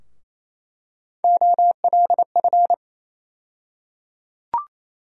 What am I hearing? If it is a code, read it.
Morse code, heard as a keyed tone, about 28 wpm, 700 Hz. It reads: OLF
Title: USA Call Sign Postfixes - 3 Letters 28wpm